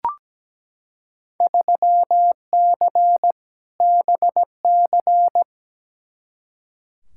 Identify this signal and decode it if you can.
Morse code, heard as a keyed tone, about 17 wpm, 700 Hz. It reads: 3C BC
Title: Field Day Exchanges Encore - 17wpm